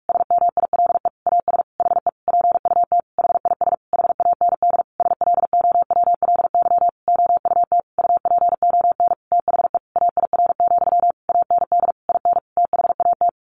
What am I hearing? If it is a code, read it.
Morse code, heard as a keyed tone, about 44 wpm, 700 Hz. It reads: SMILE AS HE PUT HIS HAND SLOWLY OUT UPON THE AIR, AND IN THAT
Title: A Sound Of Thunder 45wpm